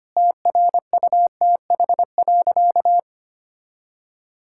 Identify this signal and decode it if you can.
Morse code, heard as a keyed tone, about 25 wpm, 700 Hz. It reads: TRUTH.
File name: practice / 25wpm / Sentences with 9-Word Phrases - 25wpm